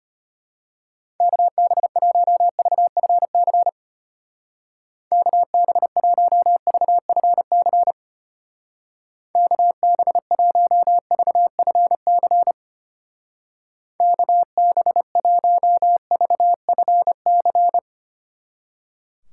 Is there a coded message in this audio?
Yes: Morse code from 1.2 to 17.8 s, heard as a keyed tone, about 31 wpm, 700 Hz. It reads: KB1VFC KB1VFC KB1VFC KB1VFC